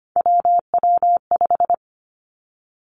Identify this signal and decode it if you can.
Morse code, heard as a keyed tone, about 25 wpm, 700 Hz. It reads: WW5
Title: Three Challenging Letter-Number Combinations 25wpm